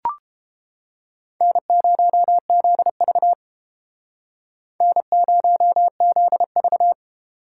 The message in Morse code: N0ZV N0ZV

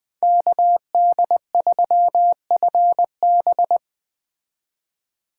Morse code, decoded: KD3FB

